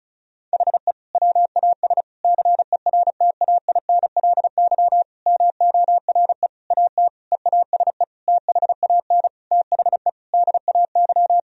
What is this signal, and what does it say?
Morse code, heard as a keyed tone, about 34 wpm, 700 Hz. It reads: HE WAS CERTAINLY MORE AT EASE THAN THE DAY
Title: Sentences from Top 1000 Words - Review Part 1 - 35wpm